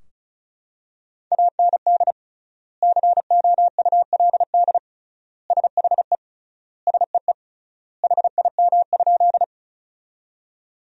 Morse code, heard as a keyed tone, about 34 wpm, 700 Hz. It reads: AND COULD SHE SEE HIM?
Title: 3x Word-Spacing - Sentences from Top 100 Words - 35wpm